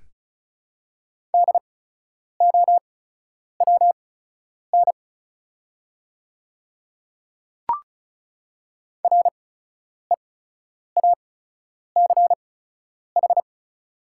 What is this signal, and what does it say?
Morse code, heard as a keyed tone, about 36 wpm, spaced as for 10 wpm, 700 Hz. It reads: DOWN REACH
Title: 8x Character Spacing - Top 500 Words - 35wpm